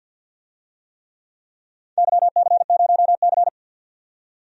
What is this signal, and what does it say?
Morse code, heard as a keyed tone, about 47 wpm, 700 Hz. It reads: YC0C
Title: DXCC-100 Call Signs - 4 Characters - 50wpm